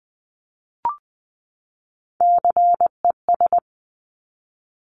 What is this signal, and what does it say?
Morse code, heard as a keyed tone, about 20 wpm, 700 Hz. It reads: CES
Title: Top Three Letter Combinations in Words 20wpm